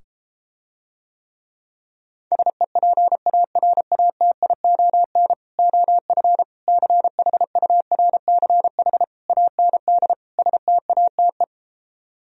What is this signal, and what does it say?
Morse code, heard as a keyed tone, about 33 wpm, 700 Hz. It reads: SEPARATION OF CHURCH AND STATE